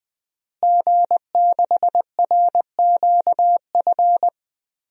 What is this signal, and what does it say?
Morse code, heard as a keyed tone, about 21 wpm, 700 Hz. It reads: G6RQF
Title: DXCC-100 Call Signs - 5 Characters - Repeated 3x - 20wpm